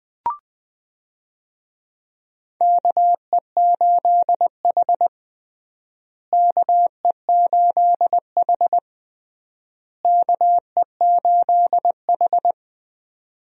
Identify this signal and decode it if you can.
Morse code, heard as a keyed tone, about 21 wpm, 700 Hz. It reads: KE8H KE8H KE8H
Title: USA Call Signs - 4 Characters - Repeated 3x - 20wpm